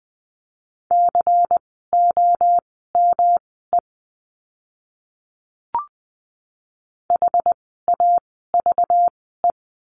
Morse code, heard as a keyed tone, about 20 wpm, spaced as for 14 wpm, 700 Hz. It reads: COME HAVE
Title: 237 Lesson 38f - Letter q - Words 20wpm